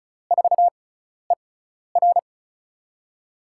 Morse code, heard as a keyed tone, about 36 wpm, spaced as for 12 wpm, 700 Hz. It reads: 4ER
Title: USA Call Signs - 4 Characters - 6x Character Spacing - 35wpm